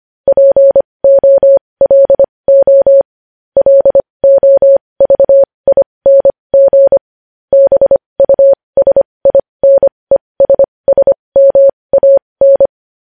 Morse code, heard as a keyed tone, about 25 wpm, spaced as for 20 wpm, 550 Hz. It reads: POLO LOVING BUSINESSMAN
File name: podcast / News headlines-1x-25x20